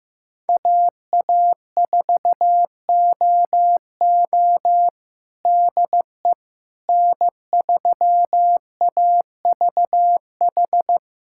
Morse code, heard as a keyed tone, about 15 wpm, 700 Hz. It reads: AA4OO DE N3AVH
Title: USA Callsign DE Callsign - 5 Characters - 15wpm